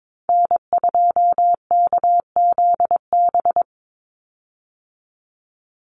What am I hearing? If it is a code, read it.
Morse code, heard as a keyed tone, about 22 wpm, 700 Hz. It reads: N2KZB
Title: USA Call Signs 22wpm